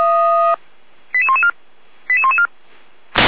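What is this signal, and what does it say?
Five-tone selective call (ZVEI), heard as quick runs of five tones, tones of about 80 ms. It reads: 88114 88114
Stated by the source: ZVEI five-tone selective call 88119 received by SDR.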